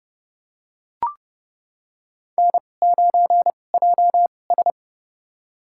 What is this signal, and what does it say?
Morse code, heard as a keyed tone, about 30 wpm, spaced as for 22 wpm, 700 Hz. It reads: N9JS
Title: USA Call Signs - 4 Characters - 2x Character Spacing - 30wpm